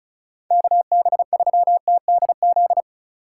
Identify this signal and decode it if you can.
Morse code, heard as a keyed tone, about 36 wpm, 700 Hz. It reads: KB3TDZ
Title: USA Call Signs - 6 Characters - 2x Character Spacing - 35wpm